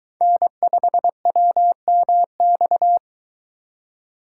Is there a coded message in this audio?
Yes: Morse code from 0.2 to 3.0 s, heard as a keyed tone, about 23 wpm, 700 Hz. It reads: N5WMX